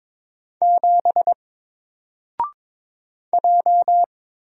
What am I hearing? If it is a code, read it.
Morse code, heard as a keyed tone, about 22 wpm, 700 Hz. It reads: 7 J
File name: Single-Letter-Number-22wpm